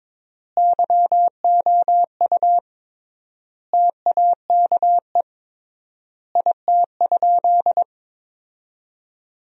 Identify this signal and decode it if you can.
Morse code, heard as a keyed tone, about 22 wpm, 700 Hz. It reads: YOU TAKE IT?
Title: 3x Word-Spacing - Sentences from Top 200 Words - 22wpm